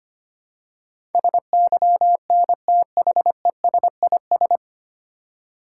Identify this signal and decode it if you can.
Morse code, heard as a keyed tone, about 25 wpm, 700 Hz. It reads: SYNTHESIS
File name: Nine Letter Words - 25wpm